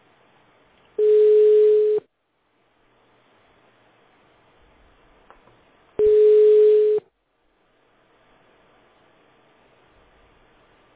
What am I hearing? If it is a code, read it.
Ringback tone (Europe, 425 Hz).